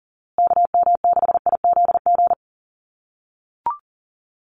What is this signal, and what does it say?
Morse code, heard as a keyed tone, about 40 wpm, 700 Hz. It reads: KM6IZG
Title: USA Call Signs Repeated 3x 40wpm